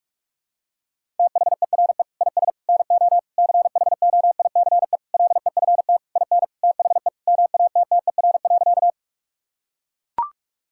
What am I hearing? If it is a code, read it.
Morse code, heard as a keyed tone, about 41 wpm, 700 Hz. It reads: THERE IS NO CHOICE LEFT IN THE MATTER.
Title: Sentences from Top 1000 Words - Review Part 6 - 45wpm